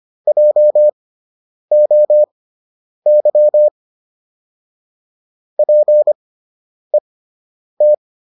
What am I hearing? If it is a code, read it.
Morse code, heard as a keyed tone, about 25 wpm, spaced as for 9 wpm, 600 Hz. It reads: JOY PET